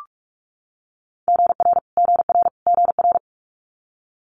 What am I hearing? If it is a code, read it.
Morse code, heard as a keyed tone, about 46 wpm, 700 Hz. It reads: GR GR GR
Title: Multi-letter Phonemes Repeated 3x 45wpm